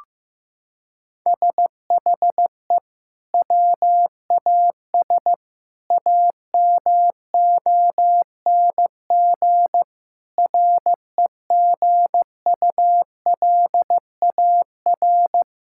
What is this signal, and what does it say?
Morse code, heard as a keyed tone, about 16 wpm, 700 Hz. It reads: SHE WAS AMONG REGULAR
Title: Sentences from Top 1000 Words - Review Part 6 - 15wpm